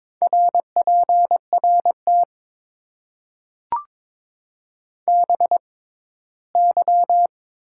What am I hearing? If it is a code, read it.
Morse code, heard as a keyed tone, about 22 wpm, spaced as for 15 wpm, 700 Hz. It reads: RPRT BY
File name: Top 90 QSO Elements - 6x Character Spacing - 22wpm